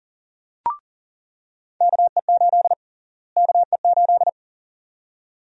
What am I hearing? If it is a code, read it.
Morse code, heard as a keyed tone, about 40 wpm, 700 Hz. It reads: KE8 KE8
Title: USA Call Sign Prefixes - Repeated 2x - 40wpm